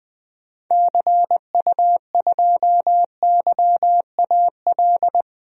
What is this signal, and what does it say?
Morse code, heard as a keyed tone, about 21 wpm, 700 Hz. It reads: CU2YAL
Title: DXCC-100 Call Signs - 6 Characters - Repeated 2x - 20wpm